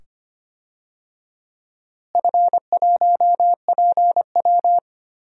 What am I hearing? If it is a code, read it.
Morse code, heard as a keyed tone, about 25 wpm, 700 Hz. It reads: F1PW
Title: DXCC-100 Call Signs - 4 Characters - Repeated 2x - 25wpm